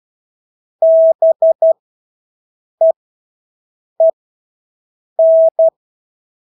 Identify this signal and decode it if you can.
Morse code, heard as a keyed tone, about 12 wpm, spaced as for 6 wpm, 650 Hz. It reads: BEEN
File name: Engelske ord 12-6 wpm